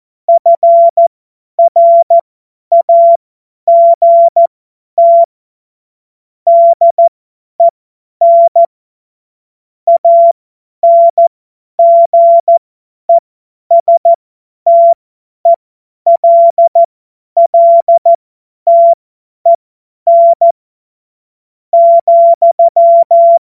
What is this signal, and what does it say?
Morse code, heard as a keyed tone, about 14 wpm, spaced as for 10 wpm, 680 Hz. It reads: FRAGT DEN ANGESTELLTEN,
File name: Morsekurs / Witze / Witz 16. Mai 2023 50 F70